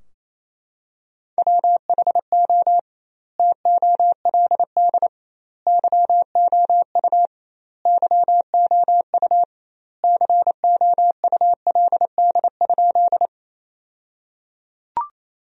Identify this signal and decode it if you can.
Morse code, heard as a keyed tone, about 28 wpm, 700 Hz. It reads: WHO TOLD YOU YOU COULD?
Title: 2x Word-Spacing - Sentences from Top 400 Words - 28wpm